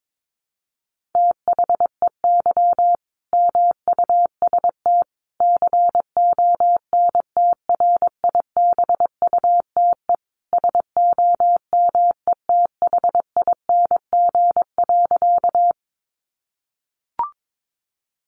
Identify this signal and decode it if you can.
Morse code, heard as a keyed tone, about 22 wpm, 700 Hz. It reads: THEY MUST CONTRIBUTE SOMETHING.
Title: Sentences from Top 1400 Words 22wpm